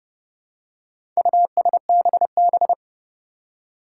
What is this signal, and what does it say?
Morse code, heard as a keyed tone, about 31 wpm, 700 Hz. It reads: USBB
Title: Four Challenging Letter-Number Combinations 30wpm